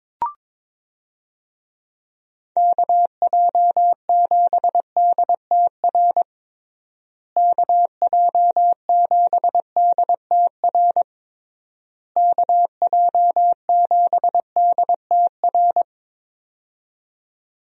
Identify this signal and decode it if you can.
Morse code, heard as a keyed tone, about 22 wpm, 700 Hz. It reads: KJ7DTR KJ7DTR KJ7DTR
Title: USA Call Signs - 6 Characters - Repeated 3x - 22wpm